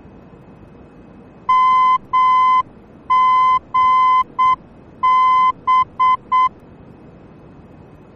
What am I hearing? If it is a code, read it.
Morse code, heard as a keyed tone, about 7 wpm, 1020 Hz. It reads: MGB
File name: germany / mx / MGB-D